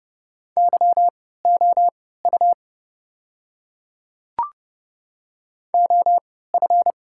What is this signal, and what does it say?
Morse code, heard as a keyed tone, about 29 wpm, spaced as for 17 wpm, 700 Hz. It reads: YOU OF
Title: 3x Character Spacing - Top 100 Words - 30wpm